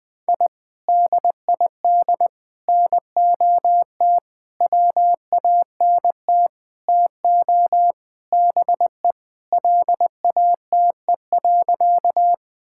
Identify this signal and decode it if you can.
Morse code, heard as a keyed tone, about 21 wpm, 700 Hz. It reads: I DID NOT WANT TO BE LATE.
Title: Sentences with 5-Word Phrases - Spoken Beginning - 20wpm